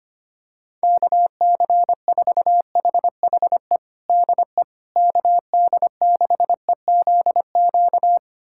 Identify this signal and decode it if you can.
Morse code, heard as a keyed tone, about 25 wpm, 700 Hz. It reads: KC4HHE DE KD6EZQ
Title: USA Callsign DE Callsign - 6 Characters - 25wpm